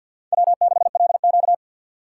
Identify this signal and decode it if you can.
Morse code, heard as a keyed tone, about 52 wpm, 700 Hz. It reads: W6LQ
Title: USA Call Signs - 6x Character Spacing - 50wpm